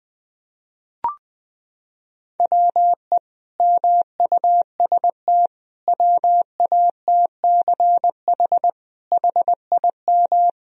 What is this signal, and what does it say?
Morse code, heard as a keyed tone, about 20 wpm, 700 Hz. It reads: WE MUST WATCH HIM
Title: Sentences from Top 1000 Words - Review Part 3 - 20wpm